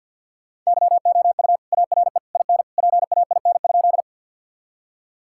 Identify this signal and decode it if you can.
Morse code, heard as a keyed tone, about 50 wpm, 700 Hz. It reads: YOU ARE IN PAIN?
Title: Sentences from Top 1600 Words 50wpm